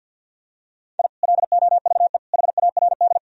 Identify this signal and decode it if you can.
Morse code, heard as a keyed tone, about 51 wpm, 700 Hz. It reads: I LOVE HARD
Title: Sentences from Top 1000 Words - Review Part 1 - 50wpm